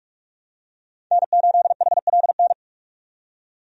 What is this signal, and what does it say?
Morse code, heard as a keyed tone, about 41 wpm, 700 Hz. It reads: N8HLN
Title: USA Call Signs - 5 Characters - 45wpm